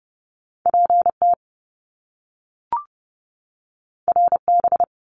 Morse code, heard as a keyed tone, about 30 wpm, 700 Hz. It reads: PT RB
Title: Top Two Letter Combinations in Words 30wpm